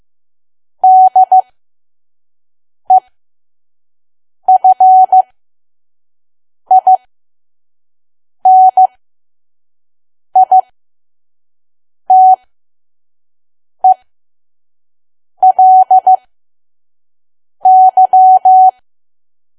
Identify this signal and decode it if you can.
Morse code, heard as a keyed tone, about 15 wpm, spaced as for 5 wpm, 750 Hz. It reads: DEFINITELY
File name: Archive / 5 WPM files / PM